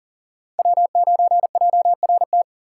Morse code, heard as a keyed tone, about 37 wpm, 700 Hz. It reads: W9JRT
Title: USA Call Signs - 5 Characters - 3x Character Spacing - 40wpm